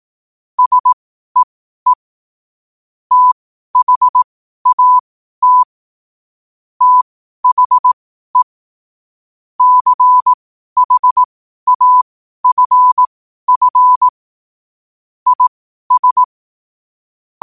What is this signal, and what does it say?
Morse code, heard as a keyed tone, about 18 wpm, spaced as for 12 wpm, 1000 Hz. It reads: SEE THAT THE CHAFF IS
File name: qotdfiles / 12wpm